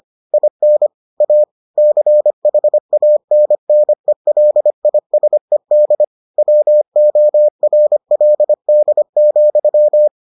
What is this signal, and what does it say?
Morse code, heard as a keyed tone, about 25 wpm, 600 Hz. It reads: IN A CHANNELISED WORLD,